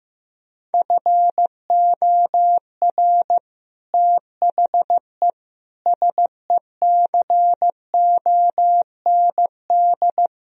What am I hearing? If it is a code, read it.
Morse code, heard as a keyed tone, about 15 wpm, 700 Hz. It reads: FOR THE SECOND
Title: Sentences from Top 1000 Words - Review Part 1 - 15wpm